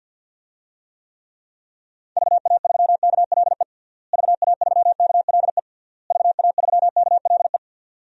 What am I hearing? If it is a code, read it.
Morse code, heard as a keyed tone, about 52 wpm, 700 Hz. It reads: VA3KLE VA3KLE VA3KLE
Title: DXCC-100 Call Signs - 6 Characters - Repeated 3x - 50wpm